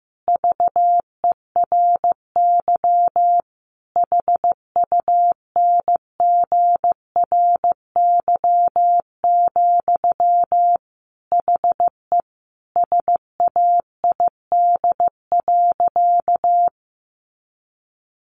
Morse code, heard as a keyed tone, about 15 wpm, 700 Hz. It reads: VERY HUNGRY, HE SAID.
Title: Sentences from Top 1400 Words 15wpm